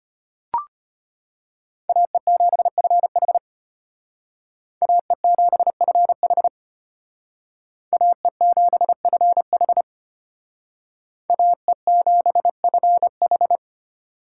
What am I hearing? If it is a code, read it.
Morse code, heard as a keyed tone, about 33 wpm, 700 Hz. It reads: AE7FH AE7FH AE7FH AE7FH